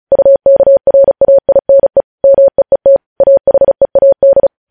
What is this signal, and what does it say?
Morse code, heard as a keyed tone, about 36 wpm, 550 Hz. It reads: UKRAINE MEET AHEAD
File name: podcast / News headlines-1x-35x35